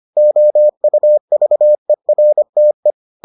Morse code, heard as a keyed tone, about 25 wpm, 600 Hz. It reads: OUVERTE
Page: bulletin UFT sous forme texte et en CW